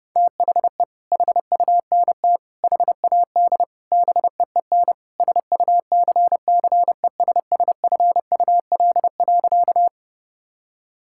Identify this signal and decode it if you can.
Morse code, heard as a keyed tone, about 30 wpm, 700 Hz. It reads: THE HUNT HAD BEEN SUCCESSFUL.
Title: Sentences from Top 1600 Words 30wpm